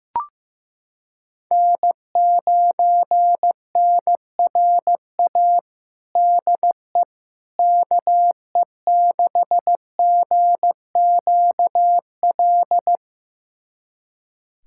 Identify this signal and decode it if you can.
Morse code, heard as a keyed tone, about 15 wpm, 700 Hz. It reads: N9NRA DE KE6GQL
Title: USA Callsign DE Callsign 15wpm